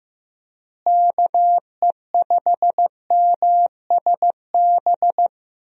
Morse code, heard as a keyed tone, about 15 wpm, 700 Hz. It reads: KE5MSB
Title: USA Call Signs - 6 Characters - Repeated 3x - 15wpm